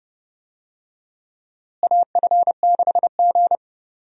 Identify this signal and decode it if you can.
Morse code, heard as a keyed tone, about 32 wpm, 700 Hz. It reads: AF6G
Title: USA Call Signs - 4 Characters - Repeated 2x - 30wpm